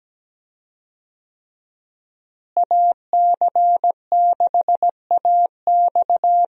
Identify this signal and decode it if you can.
Morse code, heard as a keyed tone, about 17 wpm, 700 Hz. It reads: AC6AX
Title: USA Call Signs - 5 Characters - Repeated 3x - 17wpm